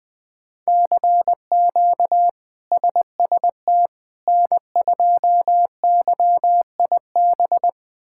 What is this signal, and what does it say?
Morse code, heard as a keyed tone, about 20 wpm, 700 Hz. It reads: CQ SST N2YIB
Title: K1USN SST QSO Practice - 20wpm